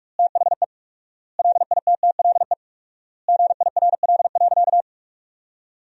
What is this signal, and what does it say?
Morse code, heard as a keyed tone, about 41 wpm, 700 Hz. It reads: THE LITTLE GIRL.
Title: 4x Word-Spacing - Sentences from Top 400 Words - 45wpm